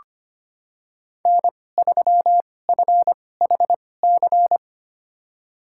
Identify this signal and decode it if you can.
Morse code, heard as a keyed tone, about 25 wpm, spaced as for 18 wpm, 700 Hz. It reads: N3FHC
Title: USA Call Signs - 5 Characters - 2x Character Spacing - 25wpm